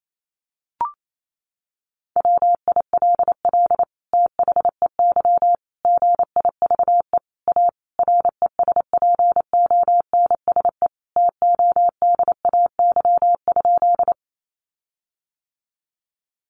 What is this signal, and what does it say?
Morse code, heard as a keyed tone, about 28 wpm, 700 Hz. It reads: WILL THEY GIVE A RESPONSE TODAY?
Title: Sentences from Top 1400 Words 28wpm